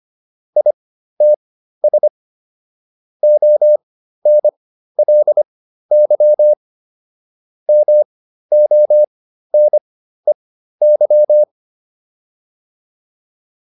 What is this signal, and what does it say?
Morse code, heard as a keyed tone, about 25 wpm, spaced as for 13 wpm, 600 Hz. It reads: ITS ONLY MONEY